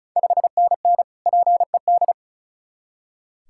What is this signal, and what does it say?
Morse code, heard as a keyed tone, about 36 wpm, 690 Hz. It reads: 5NN PED
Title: Ontario QSO Party Exchange 35wpm